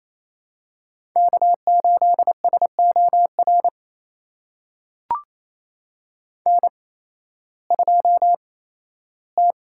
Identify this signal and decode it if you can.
Morse code, heard as a keyed tone, about 28 wpm, spaced as for 15 wpm, 700 Hz. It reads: K8SOR N2T
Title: USA Call Signs - 5 Characters - 8x Character Spacing - 28wpm